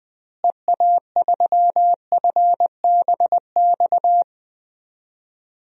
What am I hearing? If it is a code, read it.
Morse code, heard as a keyed tone, about 20 wpm, 700 Hz. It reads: EA3FBX
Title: DXCC-100 Call Signs - 6 Characters - Repeated 2x - 20wpm